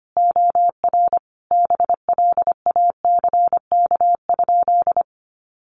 Morse code, heard as a keyed tone, about 25 wpm, 700 Hz. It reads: OR BLACK?
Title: Sentences from Top 400 Words 25wpm